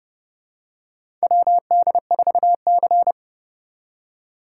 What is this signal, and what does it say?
Morse code, heard as a keyed tone, about 31 wpm, 700 Hz. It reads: WD4C
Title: USA Call Signs - 4 Characters - 3x Character Spacing - 30wpm